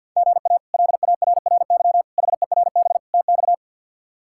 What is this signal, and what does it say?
Morse code, heard as a keyed tone, about 52 wpm, 700 Hz. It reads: GA LARRY HERB TX
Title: K1USN SST QSO Practice - 50wpm